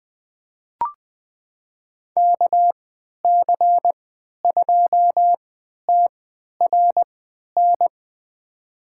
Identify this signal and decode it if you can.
Morse code, heard as a keyed tone, about 20 wpm, spaced as for 11 wpm, 700 Hz. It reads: KC2TRN
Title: USA Call Signs - 6 Characters - 3x Character Spacing - 20wpm